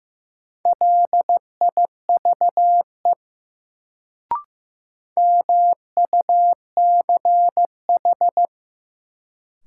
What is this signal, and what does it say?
Morse code, heard as a keyed tone, about 15 wpm, 700 Hz. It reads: LIVE MUCH
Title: Sentences from Top 200 Words - New Word Drill - 15wpm